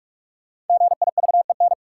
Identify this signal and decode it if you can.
Morse code, heard as a keyed tone, about 46 wpm, 700 Hz. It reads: GIVEN
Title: Sentences from Top 1400 Words - New Word Drill - 45wpm